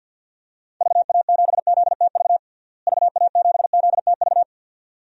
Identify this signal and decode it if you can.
Morse code, heard as a keyed tone, about 53 wpm, 700 Hz. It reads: VA7ZTV VA7ZTV
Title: DXCC-100 Call Signs - 6 Characters - Repeated 2x - 50wpm